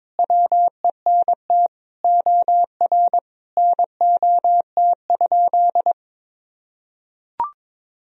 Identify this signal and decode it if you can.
Morse code, heard as a keyed tone, about 22 wpm, 700 Hz. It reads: WENT OR NOT?
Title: Sentences from Top 1000 Words - Review Part 3 - 22wpm